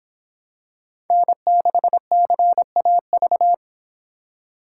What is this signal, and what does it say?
Morse code, heard as a keyed tone, about 26 wpm, 700 Hz. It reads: N6CAV